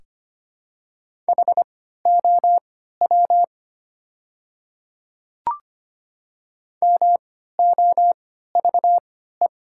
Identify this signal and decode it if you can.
Morse code, heard as a keyed tone, about 25 wpm, spaced as for 14 wpm, 700 Hz. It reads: HOW MOVE
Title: 3x Character Spacing - Top 200 Words - 25wpm